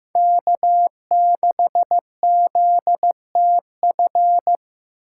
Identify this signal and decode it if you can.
Morse code, heard as a keyed tone, about 15 wpm, 700 Hz. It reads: K6ZTF
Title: USA Call Signs - 5 Characters - 4x Character Spacing - 15wpm